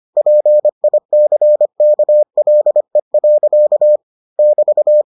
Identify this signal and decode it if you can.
Morse code, heard as a keyed tone, about 25 wpm, 600 Hz. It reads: PICKLE. <BT>